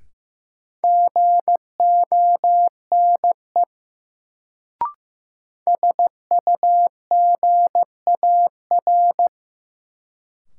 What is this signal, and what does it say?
Morse code, heard as a keyed tone, about 15 wpm, 700 Hz. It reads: GONE SUGAR
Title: Sentences from Top 1200 Words - New Word Drill - 15wpm